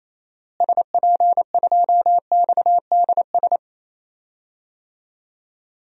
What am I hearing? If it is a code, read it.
Morse code, heard as a keyed tone, about 28 wpm, 700 Hz. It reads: SP2XDS
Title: DXCC-100 Call Signs - 6 Characters - Repeated 2x - 28wpm